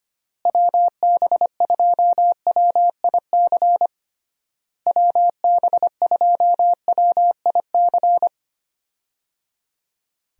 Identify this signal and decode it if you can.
Morse code, heard as a keyed tone, about 25 wpm, 700 Hz. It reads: WB2WIC WB2WIC
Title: USA Call Signs - 6 Characters - Repeated 3x - 25wpm